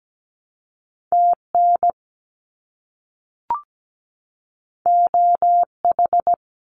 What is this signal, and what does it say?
Morse code, heard as a keyed tone, about 17 wpm, 700 Hz. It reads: TN OH
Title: US States 17wpm